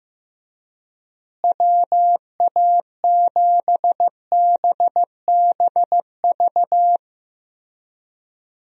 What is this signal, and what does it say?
Morse code, heard as a keyed tone, about 16 wpm, 700 Hz. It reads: WA7BBV
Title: USA Call Signs - 6 Characters - Repeated 3x - 15wpm